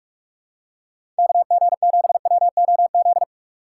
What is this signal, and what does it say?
Morse code, heard as a keyed tone, about 44 wpm, 700 Hz. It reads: KG7WOZ
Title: USA Call Signs - 6 Characters - 6x Character Spacing - 45wpm